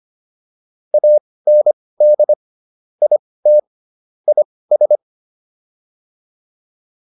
Morse code, heard as a keyed tone, about 25 wpm, spaced as for 18 wpm, 600 Hz. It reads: AND IT IS